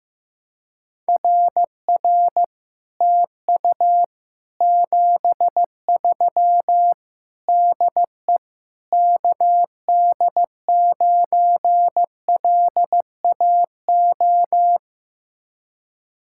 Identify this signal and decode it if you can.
Morse code, heard as a keyed tone, about 15 wpm, 700 Hz. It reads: RR TU 73 DE KD9LAO
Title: Speed-Racing Common QSO Phrases Encore - 15wpm